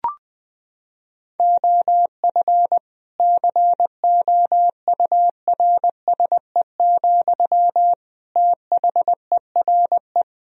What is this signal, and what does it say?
Morse code, heard as a keyed tone, about 21 wpm, 700 Hz. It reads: OF COURSE, THERE
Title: Sentences from Top 1000 Words - Review Part 4 - 20wpm